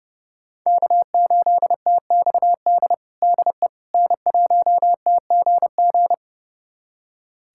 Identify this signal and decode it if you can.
Morse code, heard as a keyed tone, about 30 wpm, 700 Hz. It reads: K8TXD DE N1TGG
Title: USA Callsign DE Callsign - 5 Characters - 30wpm